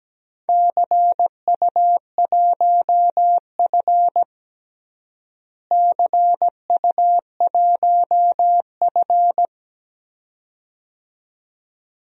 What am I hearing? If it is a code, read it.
Morse code, heard as a keyed tone, about 17 wpm, 700 Hz. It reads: CU1F CU1F
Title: DXCC-100 Call Signs - 4 Characters - Repeated 2x - 17wpm